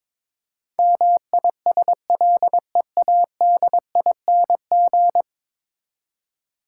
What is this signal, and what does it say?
Morse code, heard as a keyed tone, about 22 wpm, 700 Hz. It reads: MISLEADING
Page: morse-code-ninja-practice-22wpm